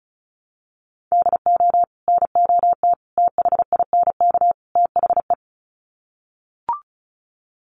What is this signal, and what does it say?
Morse code, heard as a keyed tone, about 34 wpm, 700 Hz. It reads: DO NOT THINK THE
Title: Sets of 4 Words 35wpm